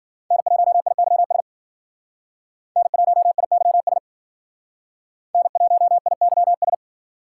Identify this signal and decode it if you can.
Morse code, heard as a keyed tone, about 52 wpm, 700 Hz. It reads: N1IYS N1IYS N1IYS